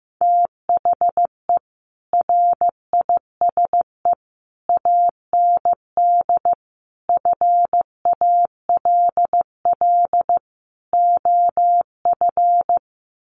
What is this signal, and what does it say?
Morse code, heard as a keyed tone, about 15 wpm, 700 Hz. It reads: THE RISE AND FALL OF
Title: Sets of 5 Words Encore2 15wpm